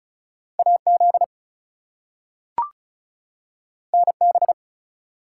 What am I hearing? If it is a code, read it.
Morse code, heard as a keyed tone, about 36 wpm, 700 Hz. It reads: AZ NB
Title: US States and Canadian Provinces 35wpm